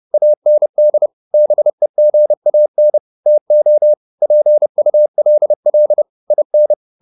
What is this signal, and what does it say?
Morse code, heard as a keyed tone, about 30 wpm, 600 Hz. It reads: AND BEGAN TO PULL IN